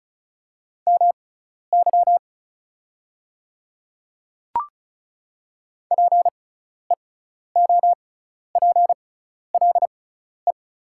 Morse code, heard as a keyed tone, about 36 wpm, spaced as for 12 wpm, 700 Hz. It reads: MY PEOPLE
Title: 6x Character Spacing - Top 100 Words - 35wpm